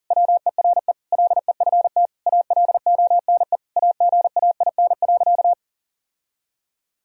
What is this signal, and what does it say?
Morse code, heard as a keyed tone, about 40 wpm, 700 Hz. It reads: WERE LEFT ALONE AGAIN.
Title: Sentences from Top 1000 Words - Review Part 1 - 40wpm